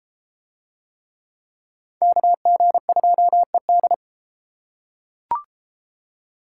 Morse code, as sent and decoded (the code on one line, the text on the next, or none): -.- --. ..--- . -..
KG2ED